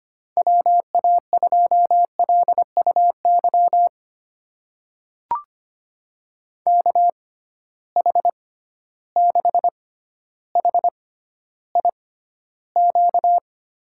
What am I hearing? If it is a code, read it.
Morse code, heard as a keyed tone, about 25 wpm, spaced as for 13 wpm, 700 Hz. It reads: WA2LUY KH6HIQ
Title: USA Call Signs - 6 Characters - 6x Character Spacing - 25wpm